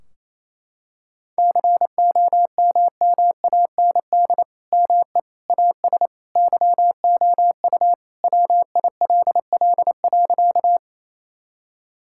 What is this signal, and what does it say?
Morse code, heard as a keyed tone, about 27 wpm, 700 Hz. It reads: COMMAND ME AS YOU WILL.
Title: Sentences from Top 1600 Words 28wpm